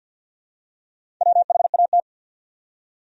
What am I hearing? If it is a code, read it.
Morse code, heard as a keyed tone, about 54 wpm, 700 Hz. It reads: WHAT
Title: Top 100 Words 50wpm